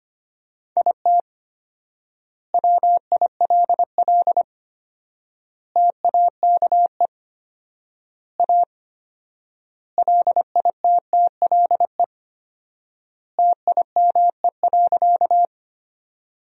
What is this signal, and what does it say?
Morse code, heard as a keyed tone, about 25 wpm, 700 Hz. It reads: IT WILL TAKE A LITTLE TIME.
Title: 4x Word-Spacing - Sentences from Top 200 Words - 25wpm